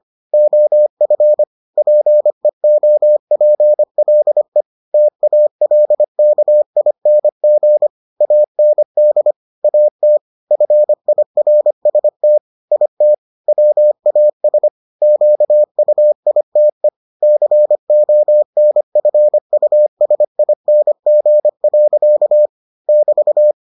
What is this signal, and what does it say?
Morse code, heard as a keyed tone, about 25 wpm, 600 Hz. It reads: OF PEOPLE TALKING AND AT FIRST IT WAS QUITE CONFUSING. <BT>